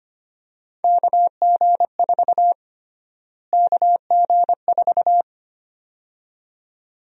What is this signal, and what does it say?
Morse code, heard as a keyed tone, about 25 wpm, 700 Hz. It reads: KG4 KG4
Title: USA Call Sign Prefixes - Repeated 2x - 25wpm